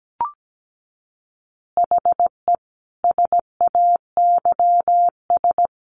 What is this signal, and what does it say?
Morse code, heard as a keyed tone, about 17 wpm, 700 Hz. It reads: HE SAYS
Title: Easy Phrases 17wpm